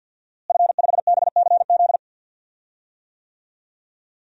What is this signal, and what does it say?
Morse code, heard as a keyed tone, about 50 wpm, 700 Hz. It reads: F5BCZ
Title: DXCC-100 Call Signs - 5 Characters - Repeated 3x - 50wpm